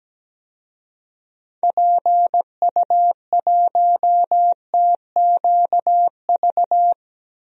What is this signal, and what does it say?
Morse code, heard as a keyed tone, about 17 wpm, 700 Hz. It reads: PU1TQV
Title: DXCC-100 Call Signs - 6 Characters - 17wpm